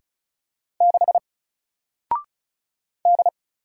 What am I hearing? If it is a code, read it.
Morse code, heard as a keyed tone, about 34 wpm, 700 Hz. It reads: 6 D
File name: Single-Letter-Number-35wpm